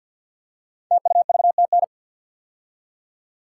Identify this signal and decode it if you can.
Morse code, heard as a keyed tone, about 48 wpm, 700 Hz. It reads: TUVTN
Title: Code Groups - 5 Letters - 50wpm